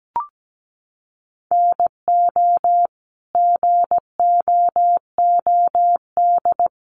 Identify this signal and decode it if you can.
Morse code, heard as a keyed tone, about 17 wpm, 700 Hz. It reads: NO GOOD
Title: Sentences from Top 800 Words 17wpm